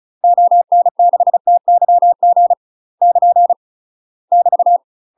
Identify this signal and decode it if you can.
Morse code, heard as a keyed tone, about 34 wpm, 700 Hz. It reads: ON6TYG <KN> <BT>
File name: UK MASTER QSOs TRAINING / 35WPM / UK MASTER QSOs 0043